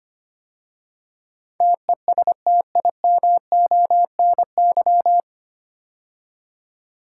Morse code, heard as a keyed tone, about 25 wpm, 700 Hz. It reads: TESTIMONY
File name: practice / 25wpm / 9 Letter Words - First 3 Letters Given - 25wpm